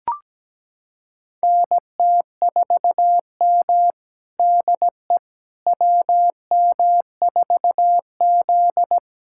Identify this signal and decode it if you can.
Morse code, heard as a keyed tone, about 17 wpm, 700 Hz. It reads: NT4M DE WM4Z
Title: USA Callsign DE Callsign - 4 Characters - 17wpm